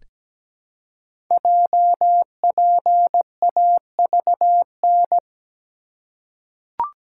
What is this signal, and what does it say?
Morse code, heard as a keyed tone, about 17 wpm, 700 Hz. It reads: JPAVN
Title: Code Groups - 5 Letters - 17wpm